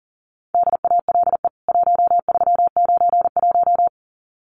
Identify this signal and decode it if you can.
Morse code, heard as a keyed tone, about 39 wpm, 700 Hz. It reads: DALE 1391
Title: CWT Exchange 40wpm